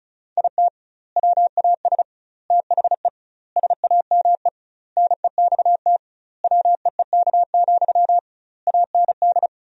Morse code, heard as a keyed tone, about 36 wpm, 700 Hz. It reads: IT WAS THE SAME NEXT WEEK, AND
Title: 2x Word-Spacing - Sentences from Top 500 Words - 35wpm